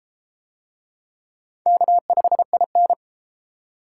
Morse code, heard as a keyed tone, about 31 wpm, 700 Hz. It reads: K5IN